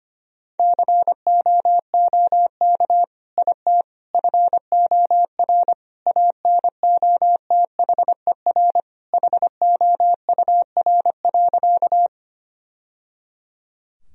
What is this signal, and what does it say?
Morse code, heard as a keyed tone, about 25 wpm, 700 Hz. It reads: COOK IT FOR ANOTHER HOUR.
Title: Sentences from Top 1000 Words - Review Part 4 - 25wpm